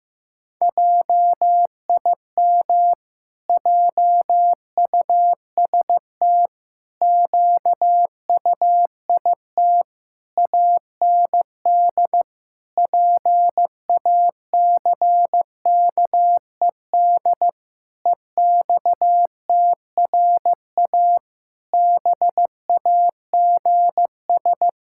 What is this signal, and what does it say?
Morse code, heard as a keyed tone, about 15 wpm, 700 Hz. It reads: JIM JUST QUIT AND PACKED EXTRA BAGS
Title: Pangram Challenge v2 - 15wpm